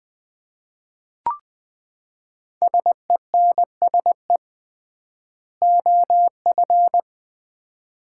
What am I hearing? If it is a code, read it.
Morse code, heard as a keyed tone, about 20 wpm, 700 Hz. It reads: SENSE OF
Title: 3x Word-Spacing - Sets of 2 Words - 20wpm